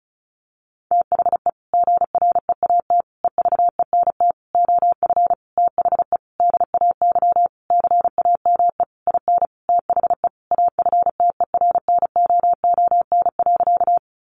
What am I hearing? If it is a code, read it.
Morse code, heard as a keyed tone, about 34 wpm, 700 Hz. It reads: THE GREAT EVENT OF THE DAY CAME IN THE AFTERNOON.
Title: Sentences from Top 1500 Words 35wpm